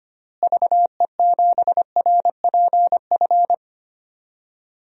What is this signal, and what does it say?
Morse code, heard as a keyed tone, about 25 wpm, 700 Hz. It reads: VE7RPF
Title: DXCC-100 Call Signs - 6 Characters - Repeated 2x - 25wpm